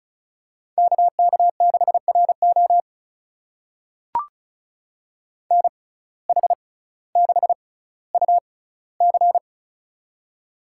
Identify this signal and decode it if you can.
Morse code, heard as a keyed tone, about 34 wpm, spaced as for 18 wpm, 700 Hz. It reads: KK6RO NH6UC
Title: USA Call Signs - 5 Characters - 6x Character Spacing - 35wpm